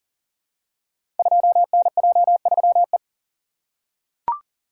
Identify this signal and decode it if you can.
Morse code, heard as a keyed tone, about 38 wpm, 700 Hz. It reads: 2NJ3E
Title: Code Groups - 5 Letter:Numbers - 40wpm